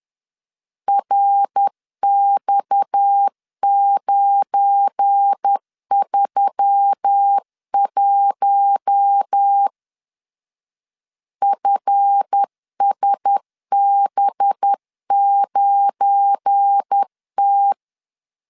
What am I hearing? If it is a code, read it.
Morse code, heard as a keyed tone, about 10 wpm, 800 Hz. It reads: RX931 FSB9T
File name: LEZIONI / 7 GF9Z / 7 GF9 10wpm 2747